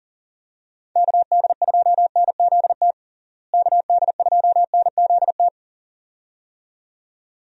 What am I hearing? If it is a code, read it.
Morse code, heard as a keyed tone, about 40 wpm, 700 Hz. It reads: KD2NZT KD2NZT
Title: USA Call Signs - 6 Characters - Repeated 2x - 40wpm